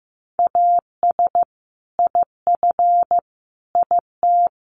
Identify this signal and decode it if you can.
Morse code, heard as a keyed tone, about 15 wpm, 700 Hz. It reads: AS IF IT
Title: Sets of 5 Words Encore2 15wpm